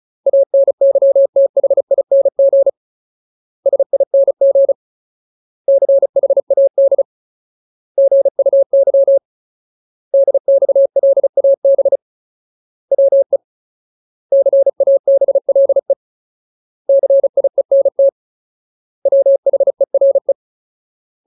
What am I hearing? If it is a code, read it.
Morse code, heard as a keyed tone, about 34 wpm, 550 Hz. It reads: ANYTHING SING CHAD GUY DXLAB WE CABLE CIENT WHERE
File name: Ultimate Word List / 35WPM Ultimate